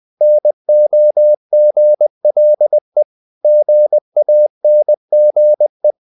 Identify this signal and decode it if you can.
Morse code, heard as a keyed tone, about 20 wpm, 600 Hz. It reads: NOGLE GANGE
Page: Ferie 20 wpm